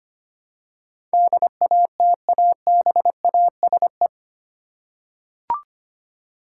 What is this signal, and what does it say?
Morse code, heard as a keyed tone, about 25 wpm, 700 Hz. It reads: DATABASE
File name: 8 Letter Words - First 3 Letters Given - 25wpm